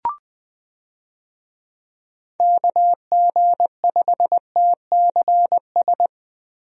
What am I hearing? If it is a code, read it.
Morse code, heard as a keyed tone, about 20 wpm, 700 Hz. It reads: KG5TCS
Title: USA Call Signs - 6 Characters - Repeated 2x - 20wpm